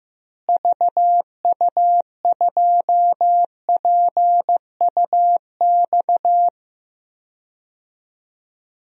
Morse code, heard as a keyed tone, about 15 wpm, 700 Hz. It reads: VU2PUX
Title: DXCC-100 Call Signs - 6 Characters - Repeated 3x - 15wpm